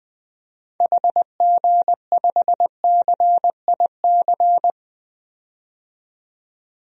Morse code, heard as a keyed tone, about 21 wpm, 700 Hz. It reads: HG5CIC
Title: DXCC-100 Call Signs - 6 Characters - 20wpm